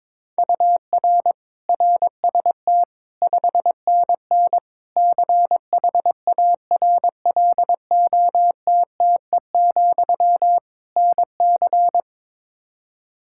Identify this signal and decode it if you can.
Morse code, heard as a keyed tone, about 22 wpm, 700 Hz. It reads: UR RST 5NN CHARLOTTE, NC
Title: K1USN SST QSO Practice 22wpm